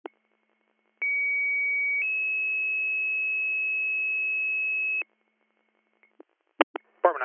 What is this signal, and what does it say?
Two-tone page, heard as a one-second tone, then a three-second tone, about 2250 then 2600 Hz, with speech. It is heard from 1.0 to 5.0 s.